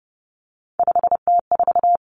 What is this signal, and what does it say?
Morse code, heard as a keyed tone, about 30 wpm, 700 Hz. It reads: 5T4
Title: Three Challenging Letter-Number Combinations 30wpm